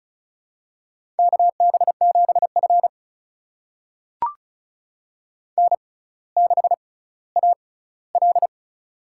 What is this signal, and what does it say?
Morse code, heard as a keyed tone, about 36 wpm, spaced as for 18 wpm, 700 Hz. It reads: KB7F N6AL
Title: USA Call Signs - 4 Characters - 6x Character Spacing - 35wpm